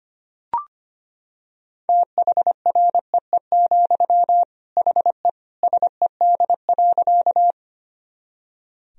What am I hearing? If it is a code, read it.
Morse code, heard as a keyed tone, about 25 wpm, 700 Hz. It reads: THREE, HE SED.
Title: Sentences from Top 200 Words and Abbreviations 25wpm